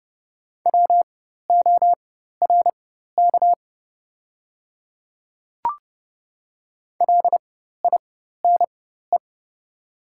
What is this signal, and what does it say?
Morse code, heard as a keyed tone, about 30 wpm, spaced as for 14 wpm, 700 Hz. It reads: WORK LINE
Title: 4x Character Spacing - Top 200 Words - 30wpm